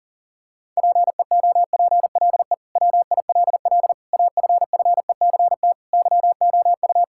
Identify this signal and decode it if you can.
Morse code, heard as a keyed tone, about 40 wpm, 700 Hz. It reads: PEOPLE WILL AFFECT YOU
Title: Sentences from Top 1000 Words - Review Part 6 - 40wpm